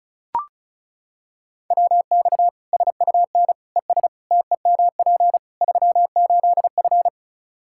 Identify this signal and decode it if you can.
Morse code, heard as a keyed tone, about 36 wpm, 700 Hz. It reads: WX SUN ES TEMP 38F
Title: Practice QSO - Exchange 2 - Info - 35wpm